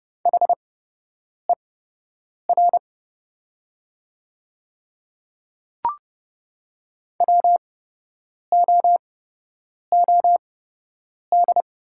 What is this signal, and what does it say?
Morse code, heard as a keyed tone, about 31 wpm, spaced as for 8 wpm, 700 Hz. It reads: HER WOOD
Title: 8x Character Spacing - Top 300 Words - 30wpm